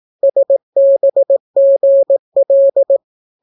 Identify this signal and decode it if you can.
Morse code, heard as a keyed tone, about 18 wpm, 550 Hz. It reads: SBGL